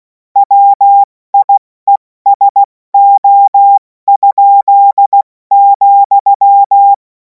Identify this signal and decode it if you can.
Morse code, heard as a keyed tone, about 16 wpm, 800 Hz. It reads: WIESO?,
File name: Morsekurs / Witze / Witz 07. April 2026 70 F80